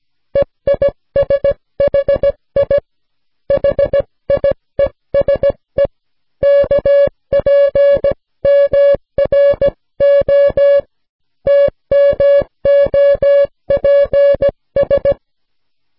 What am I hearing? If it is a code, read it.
Morse code, heard as a keyed tone, about 16 wpm, 560 Hz. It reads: EISHI HIESE KPMRO TMOPS